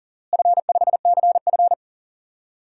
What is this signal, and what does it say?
Morse code, heard as a keyed tone, about 40 wpm, 700 Hz. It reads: F5CF
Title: DXCC-100 Call Signs - 4 Characters - Repeated 3x - 40wpm